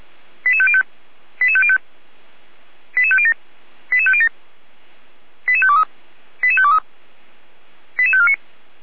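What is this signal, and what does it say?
Five-tone selective call (ZVEI), heard as quick runs of five tones, tones of about 70 ms. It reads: 88505 88505 88507 88507 88523 88523 88549
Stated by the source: ZVEI five-tone selective call 88405, received by SDR.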